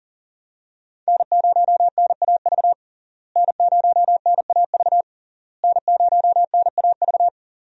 Morse code, heard as a keyed tone, about 37 wpm, 700 Hz. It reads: N0NAV N0NAV N0NAV
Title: USA Call Signs - 5 Characters - Repeated 3x - 40wpm